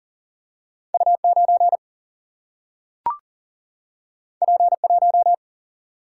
Morse code, heard as a keyed tone, about 40 wpm, 700 Hz. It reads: U 9 P 1
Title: Single-Letter Single-Number 40wpm